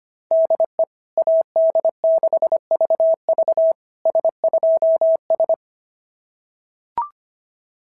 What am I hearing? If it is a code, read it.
Morse code, heard as a keyed tone, about 24 wpm, 650 Hz. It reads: DE AD6VV S2S
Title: SOTA QSO Practice 25wpm